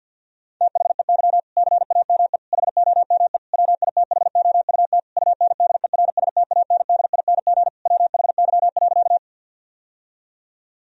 Morse code, heard as a keyed tone, about 52 wpm, 700 Hz. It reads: THEY CAME HOME WITHOUT UNDERSTANDING WHY.
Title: Sentences from Top 1000 Words - Review Part 3 - 50wpm